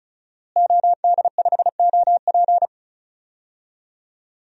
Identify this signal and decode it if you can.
Morse code, heard as a keyed tone, about 34 wpm, 700 Hz. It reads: OD5OP
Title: DXCC-100 Call Signs - 5 Characters - Repeated 2x - 35wpm